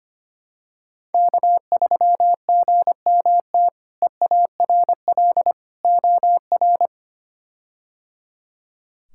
Morse code, heard as a keyed tone, about 25 wpm, 700 Hz. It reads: K3GMT EARL OR
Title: K1USN SST QSO Practice - 25wpm